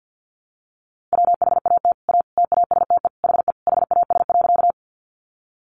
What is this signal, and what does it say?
Morse code, heard as a keyed tone, about 50 wpm, 700 Hz. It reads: WHAT A TASTE HE HAS.
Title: Sentences from Top 1100 Words 50wpm